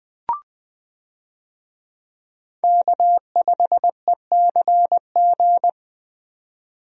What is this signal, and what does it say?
Morse code, heard as a keyed tone, about 20 wpm, 700 Hz. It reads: K5ECG
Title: USA Call Signs - 5 Characters - 20wpm